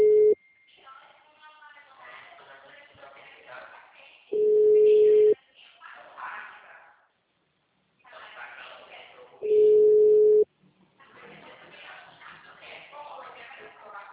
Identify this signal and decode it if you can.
Ringback tone (Europe, 425 Hz), with speech.
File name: out- -2002- - - .84514